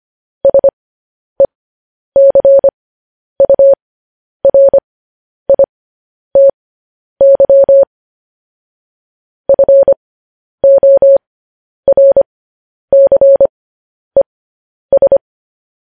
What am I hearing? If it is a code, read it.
Morse code, heard as a keyed tone, about 25 wpm, spaced as for 10 wpm, 550 Hz. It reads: SECURITY FORCES